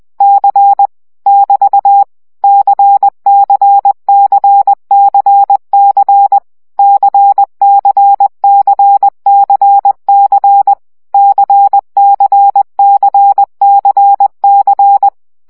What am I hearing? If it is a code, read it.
Morse code, heard as a keyed tone, about 20 wpm, 800 Hz. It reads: C <BT> CCCCC CCCCC CCCCC
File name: CW gyakorlas c 17 baud